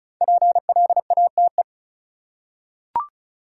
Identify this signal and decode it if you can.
Morse code, heard as a keyed tone, about 34 wpm, 700 Hz. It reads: PLATE
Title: Five Letter Words - 35wpm